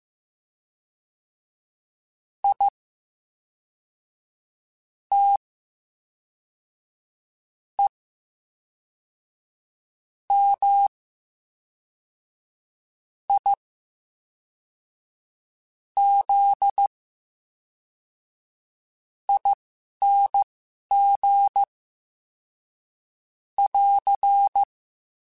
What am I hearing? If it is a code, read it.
Morse code, heard as a keyed tone, about 15 wpm, spaced as for 4 wpm, 800 Hz. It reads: ITEMIZING <AR>
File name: Special INGs / Special INGs-mp3s / VerySlowToFaster Ing @10